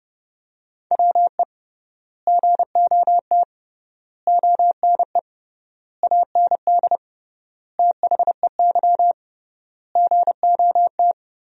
Morse code, heard as a keyed tone, about 30 wpm, 700 Hz. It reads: WE GOT ONE AND THEY GOT
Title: 3x Word-Spacing - Sentences from Top 300 Words - 30wpm